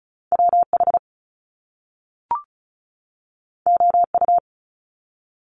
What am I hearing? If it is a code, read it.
Morse code, heard as a keyed tone, about 36 wpm, 700 Hz. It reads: WH OU
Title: Top Two Letter Combinations in Words 35wpm